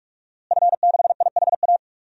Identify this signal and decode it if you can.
Morse code, heard as a keyed tone, about 46 wpm, 700 Hz. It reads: F6IHA
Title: DXCC-100 Call Signs - 5 Characters - Repeated 3x - 45wpm